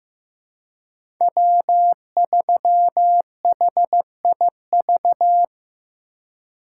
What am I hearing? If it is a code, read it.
Morse code, heard as a keyed tone, about 15 wpm, 700 Hz. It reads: W3HIV
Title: USA Call Signs - 5 Characters - 2x Character Spacing - 15wpm